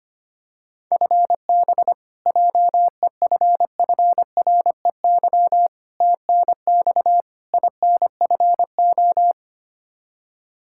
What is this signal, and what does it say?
Morse code, heard as a keyed tone, about 25 wpm, 700 Hz. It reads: FB JEFFREY TNX INFO
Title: K1USN SST QSO Practice 25wpm